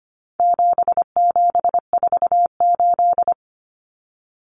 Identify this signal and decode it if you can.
Morse code, heard as a keyed tone, about 25 wpm, 700 Hz. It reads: 7748
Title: Common Cut Numbers - 4-Digit 25wpm